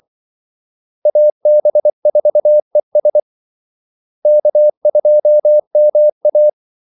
Morse code, heard as a keyed tone, about 24 wpm, 600 Hz. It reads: AB4ES K2MA
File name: Callsigns-24wpm